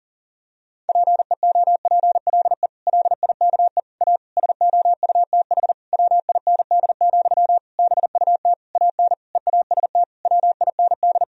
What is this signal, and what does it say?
Morse code, heard as a keyed tone, about 40 wpm, 700 Hz. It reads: PEOPLE LIKE A SOUTH WIND, BUT AN EAST WIND
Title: Sentences from Top 1000 Words - Review Part 3 - 40wpm